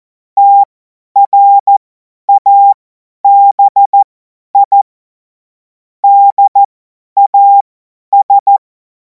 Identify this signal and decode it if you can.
Morse code, heard as a keyed tone, about 14 wpm, spaced as for 10 wpm, 800 Hz. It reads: TRABI DAS
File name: Morsekurs / Witze / Witz 02. Juni 2026 50 F70